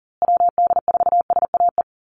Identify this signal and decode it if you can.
Morse code, heard as a keyed tone, about 40 wpm, 700 Hz. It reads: WD4SAE
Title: USA Call Signs Repeated 3x 40wpm